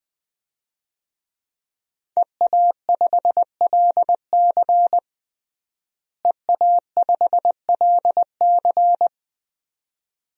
Morse code, heard as a keyed tone, about 20 wpm, 700 Hz. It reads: EA5LC EA5LC
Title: DXCC-100 Call Signs - 5 Characters - Repeated 2x - 20wpm